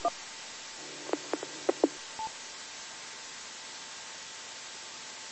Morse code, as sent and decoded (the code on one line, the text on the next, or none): none